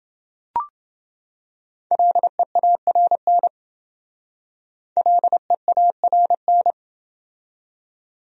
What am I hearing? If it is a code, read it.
Morse code, heard as a keyed tone, about 28 wpm, 700 Hz. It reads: LEARN LEARN